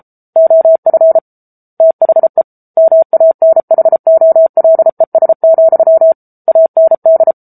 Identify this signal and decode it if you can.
Morse code, heard as a keyed tone, about 33 wpm, 650 Hz. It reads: OF THE MANHOLES, AND
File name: Book Two / 07-Chapter 7 - 33 WPM